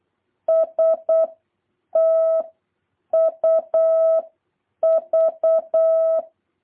Morse code, heard as a keyed tone, about 8 wpm, 650 Hz. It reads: S T U V
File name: jh8cbh / 2023cw